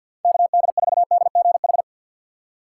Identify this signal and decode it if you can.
Morse code, heard as a keyed tone, about 53 wpm, 700 Hz. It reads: KD4DGH
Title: USA Call Signs - 3x Character Spacing - 50wpm